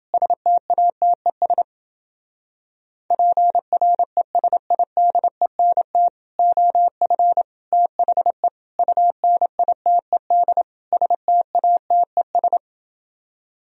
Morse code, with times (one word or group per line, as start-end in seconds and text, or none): 0.1-1.6 STATES
3.1-6.1 PRESIDENT
6.4-7.4 OF
7.7-8.5 THE
8.8-10.6 UNITED
10.9-12.6 STATES